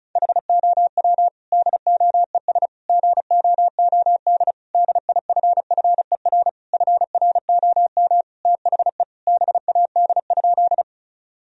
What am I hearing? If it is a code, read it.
Morse code, heard as a keyed tone, about 36 wpm, 700 Hz. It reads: HOW DOES GOOD DIFFER FROM THE BAD?
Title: Sentences from Top 1000 Words - Review Part 1 - 35wpm